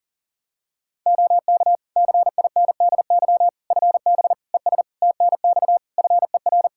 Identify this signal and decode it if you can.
Morse code, heard as a keyed tone, about 40 wpm, 700 Hz. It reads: OK CINDY FB ES TNX FER
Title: K1USN SST QSO Practice 40wpm